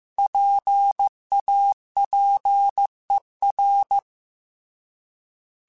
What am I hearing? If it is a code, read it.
Morse code, heard as a keyed tone, about 15 wpm, 800 Hz. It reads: PAPER
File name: Alphabetical / Alphabetical mp3s / Ps @13